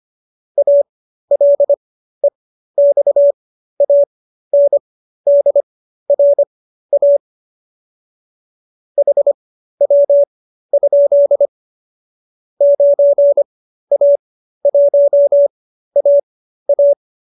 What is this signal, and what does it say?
Morse code, heard as a keyed tone, about 25 wpm, spaced as for 13 wpm, 570 Hz. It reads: ALEXANDRA HW? 9A1AA